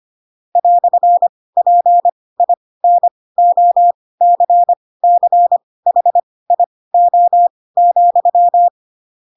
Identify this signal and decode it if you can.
Morse code, heard as a keyed tone, about 25 wpm, spaced as for 18 wpm, 700 Hz. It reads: "PINOCCHIO,
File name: Pinocchio ch3 12 25 ik1hge